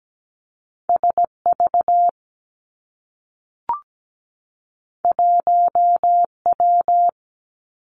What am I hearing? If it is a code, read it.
Morse code, heard as a keyed tone, about 17 wpm, 700 Hz. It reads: S V 1 W
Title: Two Challenging Letter-Number Combinations 17wpm